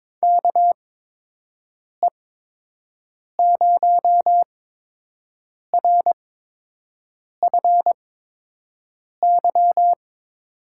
Morse code, heard as a keyed tone, about 22 wpm, spaced as for 6 wpm, 700 Hz. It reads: KE0RFY
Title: USA Call Signs - 6 Characters - 8x Character Spacing - 22wpm